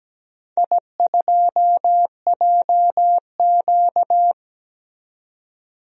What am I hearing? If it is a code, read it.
Morse code, heard as a keyed tone, about 17 wpm, 700 Hz. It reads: I2JQ
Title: DXCC-100 Call Signs - 4 Characters - 17wpm